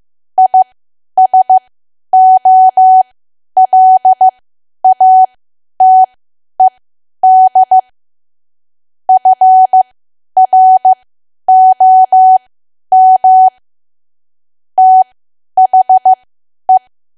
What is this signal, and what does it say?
Morse code, heard as a keyed tone, about 15 wpm, spaced as for 10 wpm, 750 Hz. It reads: ISOLATED FROM THE
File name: Archive / 10 WPM / 10WPM